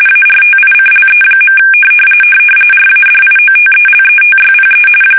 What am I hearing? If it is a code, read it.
RTTY, about 45.45 baud. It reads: ARTICLE. RADIOTELETYPE
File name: fsk850 / teletype-RTTY